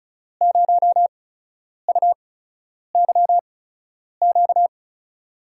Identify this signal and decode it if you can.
Morse code, heard as a keyed tone, about 34 wpm, spaced as for 10 wpm, 700 Hz. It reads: 0UYQ
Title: USA Call Signs - 5 Characters - 8x Character Spacing - 35wpm